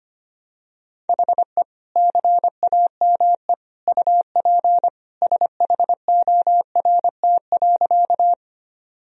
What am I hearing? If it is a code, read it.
Morse code, heard as a keyed tone, about 25 wpm, 700 Hz. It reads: HE CAME UP SHORT.
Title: Sentences from Top 1000 Words - Review Part 2 - 25wpm